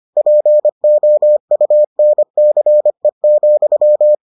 Morse code, heard as a keyed tone, about 25 wpm, 600 Hz. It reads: POUNCE,